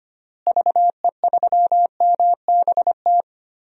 Morse code, heard as a keyed tone, about 25 wpm, 700 Hz. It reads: VE3MBT
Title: DXCC-100 Call Signs - 6 Characters - Repeated 3x - 25wpm